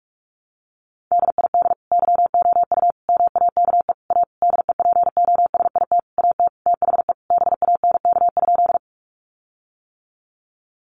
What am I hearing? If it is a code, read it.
Morse code, heard as a keyed tone, about 44 wpm, 700 Hz. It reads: DID YOU MAKE A DEPOSIT AT THE BANK?
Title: Sentences from Top 1300 Words 45wpm